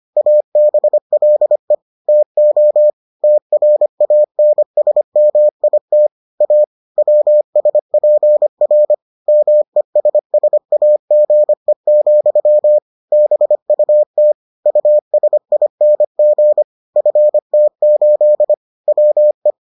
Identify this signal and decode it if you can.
Morse code, heard as a keyed tone, about 25 wpm, 600 Hz. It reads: ABLE TO TRANSMIT A WSPR MESSAGE, BUT USING FT8 WE